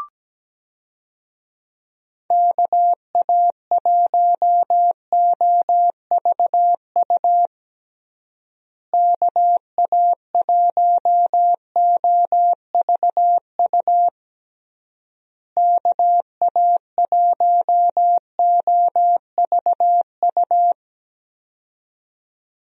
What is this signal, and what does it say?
Morse code, heard as a keyed tone, about 17 wpm, 700 Hz. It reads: KA1OVU KA1OVU KA1OVU
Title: USA Call Signs - 6 Characters - Repeated 3x - 17wpm